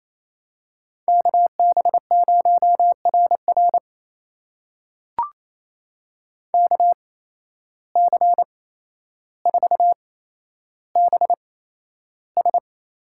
Morse code, heard as a keyed tone, about 27 wpm, spaced as for 12 wpm, 700 Hz. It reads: KB0RR KC4BS
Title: USA Call Signs - 8x Character Spacing - 28wpm